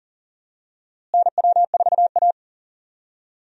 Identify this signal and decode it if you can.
Morse code, heard as a keyed tone, about 39 wpm, 700 Hz. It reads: NW4A
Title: USA Call Signs - 4 Characters - Repeated 3x - 40wpm